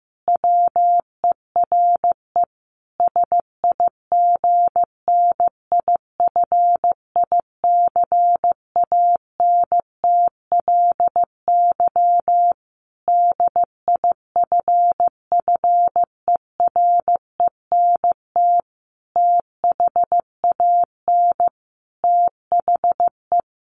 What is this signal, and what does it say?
Morse code, heard as a keyed tone, about 15 wpm, 700 Hz. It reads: WERE SIGNIFICANTLY DIFFERENT THAN THE
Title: Sentences from Top 1500 Words 15wpm